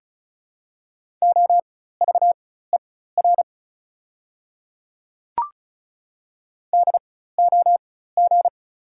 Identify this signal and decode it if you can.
Morse code, heard as a keyed tone, about 36 wpm, spaced as for 16 wpm, 700 Hz. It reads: OVER DOG
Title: 4x Character Spacing - Top 500 Words - 35wpm